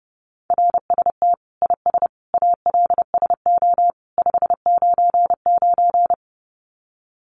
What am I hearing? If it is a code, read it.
Morse code, heard as a keyed tone, about 30 wpm, 700 Hz. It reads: RST IS ALSO 599
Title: Common QSO Phrases 30wpm